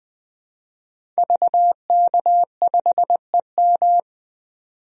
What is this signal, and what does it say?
Morse code, heard as a keyed tone, about 20 wpm, 700 Hz. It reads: VK5EM
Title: DXCC-100 Call Signs - 5 Characters - 20wpm